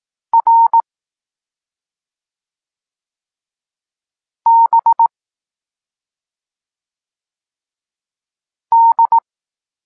Morse code, heard as a keyed tone, about 18 wpm, 920 Hz. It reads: R B D
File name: Tekst-26 1